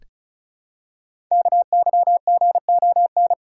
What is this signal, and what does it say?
Morse code, heard as a keyed tone, about 36 wpm, 690 Hz. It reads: KYGON
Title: Code Groups - 5 Letters - 35wpm